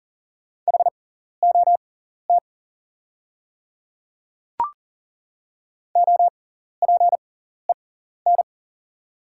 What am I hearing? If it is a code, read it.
Morse code, heard as a keyed tone, about 40 wpm, spaced as for 14 wpm, 700 Hz. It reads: HOT OPEN
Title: 6x Character Spacing - Top 300 Words - 40wpm